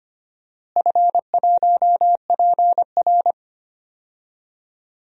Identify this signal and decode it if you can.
Morse code, heard as a keyed tone, about 25 wpm, 700 Hz. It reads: F1PR
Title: DXCC-100 Call Signs - 4 Characters - Repeated 3x - 25wpm